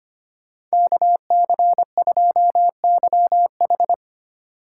Morse code, heard as a keyed tone, about 30 wpm, 700 Hz. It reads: KC2YH
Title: Speed Racing - USA 2x2 Call Signs 25wpm